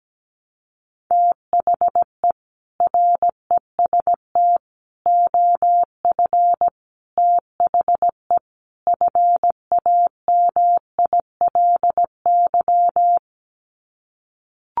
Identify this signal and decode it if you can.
Morse code, heard as a keyed tone, about 17 wpm, 700 Hz. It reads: THE REST OF THE FAMILY
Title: Sets of 5 Words Encore2 17wpm